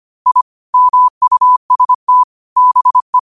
Morse code, heard as a keyed tone, about 25 wpm, 1000 Hz. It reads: I MUST BE
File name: qotdfiles / 25wpm